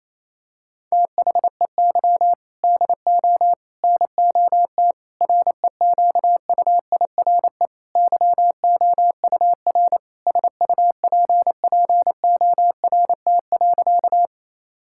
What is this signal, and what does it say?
Morse code, heard as a keyed tone, about 28 wpm, 700 Hz. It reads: THEY DO NOT REQUIRE YOUR SUPPORT.
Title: Sentences from Top 1000 Words - Review Part 6 - 28wpm